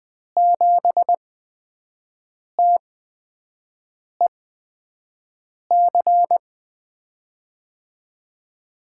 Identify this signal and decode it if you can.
Morse code, heard as a keyed tone, about 20 wpm, spaced as for 5 wpm, 700 Hz. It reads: 7TEC
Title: USA Call Signs - 5 Characters - 8x Character Spacing - 20wpm